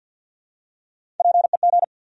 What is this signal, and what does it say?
Morse code, heard as a keyed tone, about 49 wpm, 700 Hz. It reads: PEG
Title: USA Common Names - Max 4 Letters - 50wpm